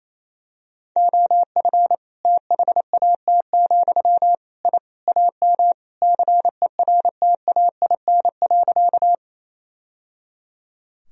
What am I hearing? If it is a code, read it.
Morse code, heard as a keyed tone, about 27 wpm, 700 Hz. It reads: OF THAT, I AM CERTAIN.
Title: Sentences from Top 1000 Words - Review Part 2 - 28wpm